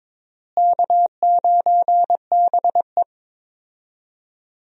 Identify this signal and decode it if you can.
Morse code, heard as a keyed tone, about 22 wpm, 700 Hz. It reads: K9BE
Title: USA Call Signs - 4 Characters - Repeated 3x - 22wpm